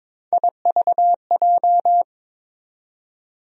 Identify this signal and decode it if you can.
Morse code, heard as a keyed tone, about 22 wpm, 700 Hz. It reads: IVJ
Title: USA Call Sign Postfixes - 3 Letters 22wpm